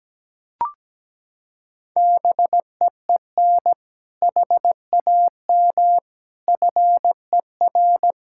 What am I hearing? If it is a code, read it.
Morse code, heard as a keyed tone, about 17 wpm, 700 Hz. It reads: BEEN HAM FER
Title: K1USN SST QSO Practice 17wpm